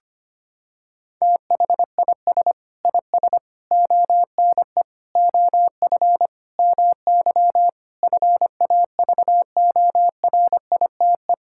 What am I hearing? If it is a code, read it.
Morse code, heard as a keyed tone, about 25 wpm, 700 Hz. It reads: THIS IS ONE OF MY FAVORITE
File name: practice / 25wpm / Sentences with 4-Word Phrases - 25wpm